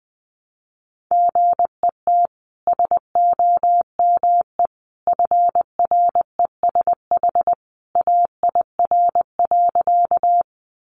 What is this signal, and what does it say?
Morse code, heard as a keyed tone, about 20 wpm, 700 Hz. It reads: GET SOME FRESH AIR.
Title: Sentences from Top 1200 Words 20wpm